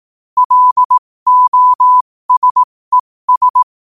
Morse code, heard as a keyed tone, about 18 wpm, spaced as for 15 wpm, 1000 Hz. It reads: LOSES
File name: qotdfiles / 15wpm